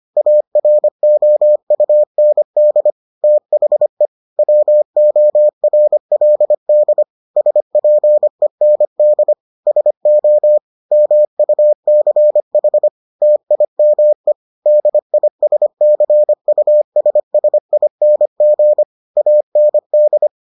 Morse code, heard as a keyed tone, about 25 wpm, 600 Hz. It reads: AROUND THE WORLD SPEND SO MUCH TIME DISCUSSING AND